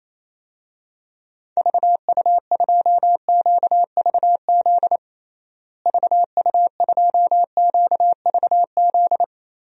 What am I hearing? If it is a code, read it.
Morse code, heard as a keyed tone, about 28 wpm, 700 Hz. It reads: VU2QVZ VU2QVZ
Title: DXCC-100 Call Signs - 6 Characters - Repeated 3x - 28wpm